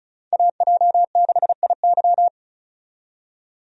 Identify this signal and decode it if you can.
Morse code, heard as a keyed tone, about 36 wpm, 700 Hz. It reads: AJ6IY
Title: USA Call Signs - 5 Characters - 35wpm